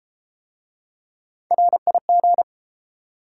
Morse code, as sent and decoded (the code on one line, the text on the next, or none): .-. .. --.
RIG